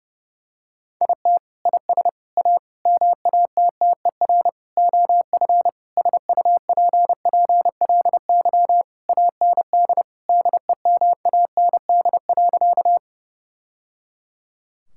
Morse code, heard as a keyed tone, about 31 wpm, 700 Hz. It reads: IT IS A MATTER OF SUPPLY AND DEMAND.
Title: Sentences from Top 1000 Words - Review Part 5 - 30wpm